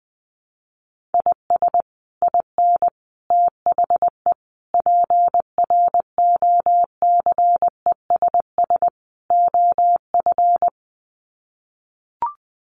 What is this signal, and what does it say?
Morse code, heard as a keyed tone, about 19 wpm, 700 Hz. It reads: IS IN THE PROCESS OF
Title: Sets of 5 Words Encore2 20wpm